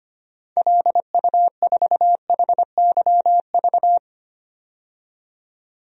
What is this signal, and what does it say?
Morse code, heard as a keyed tone, about 25 wpm, 700 Hz. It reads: LU4HYV
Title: DXCC-100 Call Signs - 6 Characters - 25wpm